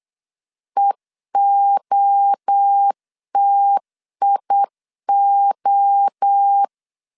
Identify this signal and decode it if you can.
Morse code, heard as a keyed tone, about 8 wpm, 800 Hz. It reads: EOTIO
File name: CW / DET / 1 3 1500C S B 8wpm 4141